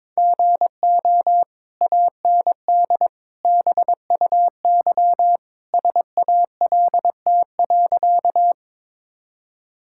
Morse code, heard as a keyed tone, about 22 wpm, 700 Hz. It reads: GO AND BUY SALT.
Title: Sentences from Top 1000 Words - Review Part 5 - 22wpm